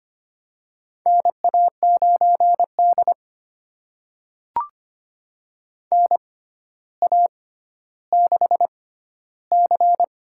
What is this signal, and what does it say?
Morse code, heard as a keyed tone, about 25 wpm, spaced as for 13 wpm, 700 Hz. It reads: NA9D NA6C
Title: USA Call Signs - 4 Characters - 6x Character Spacing - 25wpm